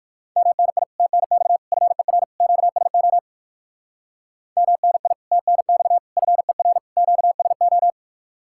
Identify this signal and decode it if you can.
Morse code, heard as a keyed tone, about 67 wpm, spaced as for 58 wpm, 700 Hz. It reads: MNI TNX FER QSO MNI TNX FER QSO